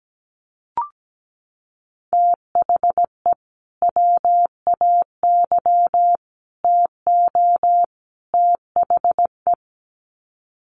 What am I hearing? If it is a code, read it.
Morse code, heard as a keyed tone, about 17 wpm, 700 Hz. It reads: THE WAY TO THE
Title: Sets of 4 Words 17wpm